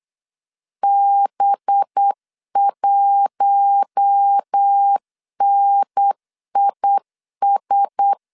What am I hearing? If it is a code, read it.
Morse code, heard as a keyed tone, about 8 wpm, 800 Hz. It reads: B1NIS
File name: LEZIONI / 6 PKX0 / 6 P 8wpm 3025